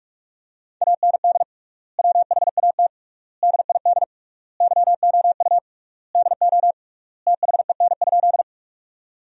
Morse code, heard as a keyed tone, about 46 wpm, 700 Hz. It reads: AND WHAT DID YOU DO THEN?
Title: 3x Word-Spacing - Sentences from Top 100 Words - 45wpm